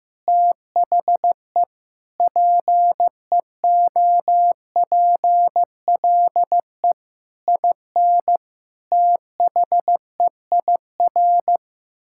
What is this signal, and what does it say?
Morse code, heard as a keyed tone, about 15 wpm, 700 Hz. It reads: THE PEOPLE IN THEIR
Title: Sentences from Top 1000 Words - Review Part 5 - 15wpm